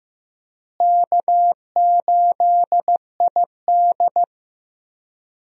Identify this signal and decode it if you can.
Morse code, heard as a keyed tone, about 15 wpm, 700 Hz. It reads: K8ID
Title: USA Call Signs - 4 Characters - Repeated 2x - 15wpm